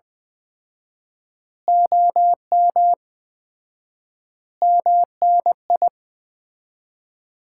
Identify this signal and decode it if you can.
Morse code, heard as a keyed tone, about 20 wpm, 700 Hz. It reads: OM MNI
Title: Common QSO Phrases - 4x Word-Spacing - 20wpm